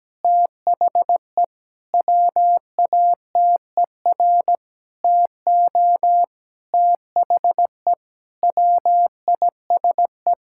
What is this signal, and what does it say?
Morse code, heard as a keyed tone, about 17 wpm, 700 Hz. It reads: THE WATER TO THE WISE
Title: Sentences from Top 1600 Words 17wpm